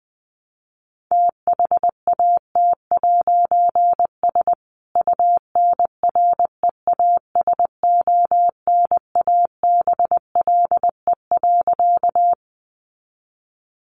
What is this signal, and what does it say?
Morse code, heard as a keyed tone, about 20 wpm, 700 Hz. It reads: THAT'S UNREASONABLE.
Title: A Sound Of Thunder 20wpm